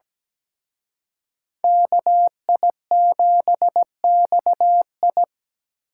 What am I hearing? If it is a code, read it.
Morse code, heard as a keyed tone, about 20 wpm, 700 Hz. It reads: KI7XI
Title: Speed Racing - USA 2x2 Call Signs 17wpm